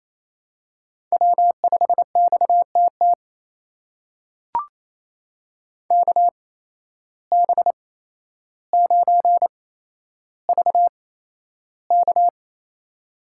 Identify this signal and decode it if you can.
Morse code, heard as a keyed tone, about 28 wpm, spaced as for 12 wpm, 700 Hz. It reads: W5XTT KB9VK
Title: USA Call Signs - 5 Characters - 8x Character Spacing - 28wpm